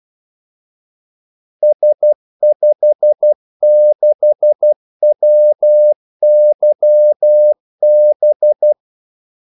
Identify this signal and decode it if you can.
Morse code, heard as a keyed tone, about 12 wpm, 600 Hz. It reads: S56WYB
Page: 2 ekstra ordmellemrum mellem kaldesignalerne